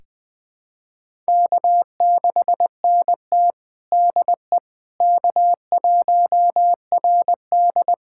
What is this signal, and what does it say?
Morse code, heard as a keyed tone, about 20 wpm, 700 Hz. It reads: K6NT DE K1RD
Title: USA Callsign DE Callsign - 4 Characters - 20wpm